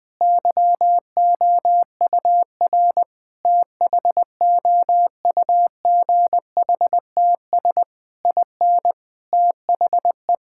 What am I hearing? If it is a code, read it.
Morse code, heard as a keyed tone, about 20 wpm, 700 Hz. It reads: YOUR THOUGHTS IN THE
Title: Sentences with 8-Word Phrases - 20wpm